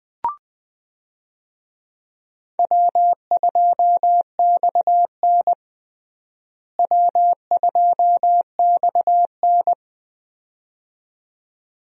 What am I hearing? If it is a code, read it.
Morse code, heard as a keyed tone, about 20 wpm, 700 Hz. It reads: W2XN W2XN
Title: USA Call Signs - 4 Characters - Repeated 2x - 20wpm